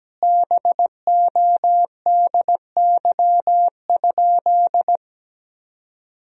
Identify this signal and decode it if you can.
Morse code, heard as a keyed tone, about 17 wpm, 700 Hz. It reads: BODY?
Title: 3x Word-Spacing - Sentences from Top 400 Words - 17wpm